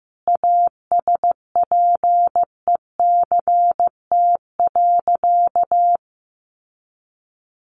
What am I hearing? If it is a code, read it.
Morse code, heard as a keyed tone, about 15 wpm, 700 Hz. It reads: ASPECT.
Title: Sentences from Top 1300 Words 15wpm